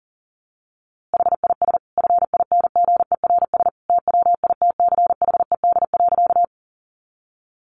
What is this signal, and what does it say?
Morse code, heard as a keyed tone, about 40 wpm, 700 Hz. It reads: HIS FINGERS TWITCHED.
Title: A Sound Of Thunder 40wpm